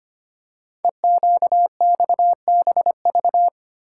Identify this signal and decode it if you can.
Morse code, heard as a keyed tone, about 25 wpm, 700 Hz. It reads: EQXBV
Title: Code Groups - 5 Letter:Numbers - 25wpm